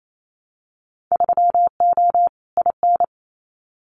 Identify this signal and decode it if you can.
Morse code, heard as a keyed tone, about 28 wpm, 700 Hz. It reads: 3O IN
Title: Winter Field Day Exchanges Encore 28wpm